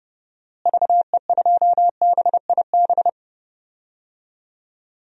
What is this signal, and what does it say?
Morse code, heard as a keyed tone, about 30 wpm, 700 Hz. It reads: VE2BIB
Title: DXCC-100 Call Signs - 6 Characters - Repeated 2x - 30wpm